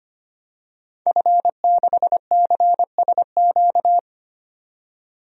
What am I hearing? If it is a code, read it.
Morse code, heard as a keyed tone, about 25 wpm, 700 Hz. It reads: F6CSQ
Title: DXCC-100 Call Signs - 5 Characters - 25wpm